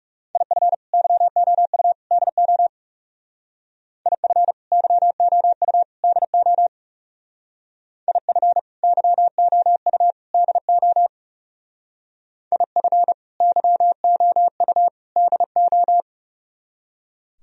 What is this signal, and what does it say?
Morse code, heard as a keyed tone, about 36 wpm, 700 Hz. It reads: IF YOU DO IF YOU DO IF YOU DO IF YOU DO
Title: Speed-Racing - Sets of 3 Words 30wpm